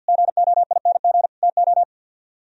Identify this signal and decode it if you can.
Morse code, heard as a keyed tone, about 55 wpm, 700 Hz. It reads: GOING TO
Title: Speed-Racing - Sets of 4 Words 50wpm